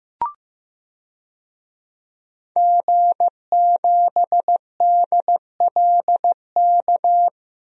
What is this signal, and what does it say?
Morse code, heard as a keyed tone, about 15 wpm, 700 Hz. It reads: G7DLK
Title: DXCC-100 Call Signs - 5 Characters - Repeated 2x - 15wpm